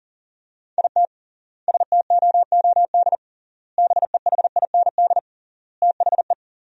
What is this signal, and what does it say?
Morse code, heard as a keyed tone, about 40 wpm, 700 Hz. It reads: IT STOOD BEHIND THE
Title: 3x Word-Spacing - Sentences from Top 500 Words - 40wpm